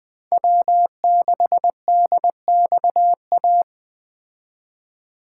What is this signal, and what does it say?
Morse code, heard as a keyed tone, about 20 wpm, 700 Hz. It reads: W6DXA